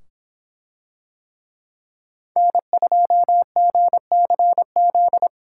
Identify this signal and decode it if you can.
Morse code, heard as a keyed tone, about 24 wpm, 700 Hz. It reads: N2GCZ